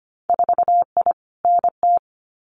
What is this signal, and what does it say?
Morse code, heard as a keyed tone, about 25 wpm, 690 Hz. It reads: 4I NT
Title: Winter Field Day Exchanges Encore 25wpm